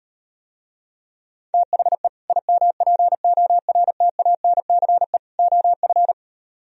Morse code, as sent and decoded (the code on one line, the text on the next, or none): - .... / .. -- .--. --- .-. - .- -. -.-. / --- ..-.
THE IMPORTANCE OF